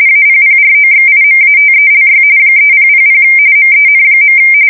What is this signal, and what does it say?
RTTY, about 45.45 baud. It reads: APPROXIMATELY 3000 MEN.
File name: fsk170 / battleshipnj